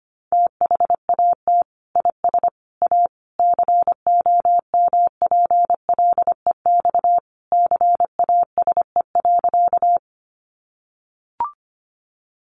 Morse code, heard as a keyed tone, about 25 wpm, 700 Hz. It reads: THAT IS A COMPLEX CASE.
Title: Sentences from Top 900 Words 25wpm